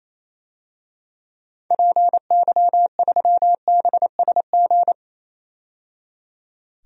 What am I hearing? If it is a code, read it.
Morse code, heard as a keyed tone, about 28 wpm, 700 Hz. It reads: PY3BSG
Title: DXCC-100 Call Signs - 6 Characters - 28wpm